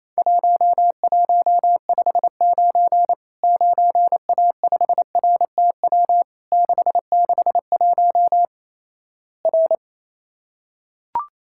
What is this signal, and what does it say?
Morse code, heard as a keyed tone, about 28 wpm, 700 Hz. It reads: 1159 9A5RTW 661 R
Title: Worked All Europe DX Contest - QTC Exchange - 28wpm